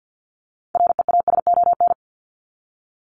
Morse code, heard as a keyed tone, about 50 wpm, 700 Hz. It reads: REASON
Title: Top 501-750 Words 50wpm